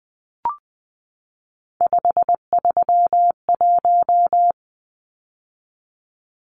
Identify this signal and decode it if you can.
Morse code, heard as a keyed tone, about 20 wpm, 700 Hz. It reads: 531
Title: Numbers - 3-Digit 20wpm